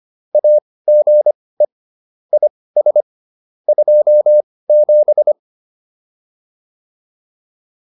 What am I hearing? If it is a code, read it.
Morse code, heard as a keyed tone, about 25 wpm, spaced as for 18 wpm, 600 Hz. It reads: AGE IS 27
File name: QSO202 18